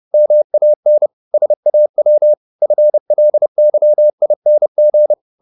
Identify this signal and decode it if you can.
Morse code, heard as a keyed tone, about 30 wpm, 600 Hz. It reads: MAN SAW FLYING
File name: the old man and the sea / the old man and the sea 1 30wpm